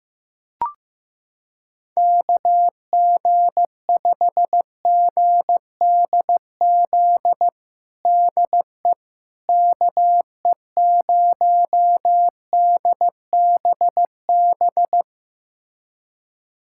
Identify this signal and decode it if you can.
Morse code, heard as a keyed tone, about 15 wpm, 700 Hz. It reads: KG5GDZ DE KE0DBB
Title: USA Callsign DE Callsign - 6 Characters - 15wpm